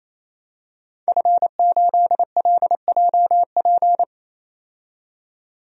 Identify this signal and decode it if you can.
Morse code, heard as a keyed tone, about 27 wpm, 700 Hz. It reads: F8LJP
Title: DXCC-100 Call Signs - 5 Characters - 28wpm